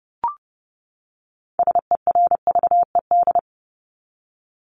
Morse code, heard as a keyed tone, about 30 wpm, 700 Hz. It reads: SERVED
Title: ED Suffix - Top 100 Words with Base Words 30wpm